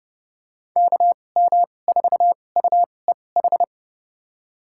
Morse code, heard as a keyed tone, about 30 wpm, spaced as for 22 wpm, 700 Hz. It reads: KM4UEH
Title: USA Call Signs - 6 Characters - 2x Character Spacing - 30wpm